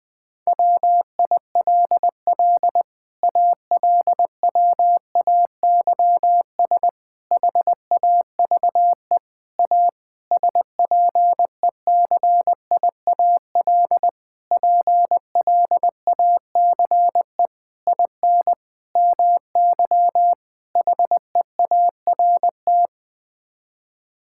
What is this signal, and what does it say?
Morse code, heard as a keyed tone, about 21 wpm, 700 Hz. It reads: WILL ALWAYS HAVE A SPECIAL PLACE IN MY HEART
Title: Sets of 9 Words - 20wpm